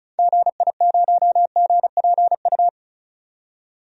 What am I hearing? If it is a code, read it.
Morse code, heard as a keyed tone, about 34 wpm, 700 Hz. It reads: GI0GPU
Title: DXCC-100 Call Signs - 6 Characters - 35wpm